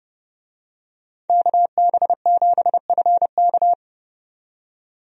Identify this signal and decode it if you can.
Morse code, heard as a keyed tone, about 31 wpm, 700 Hz. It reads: KB7FK
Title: USA Call Signs - 5 Characters - 30wpm